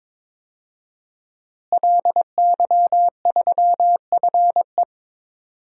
Morse code, heard as a keyed tone, about 22 wpm, 700 Hz. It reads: LY3FE
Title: DXCC-100 Call Signs - 5 Characters - Repeated 3x - 22wpm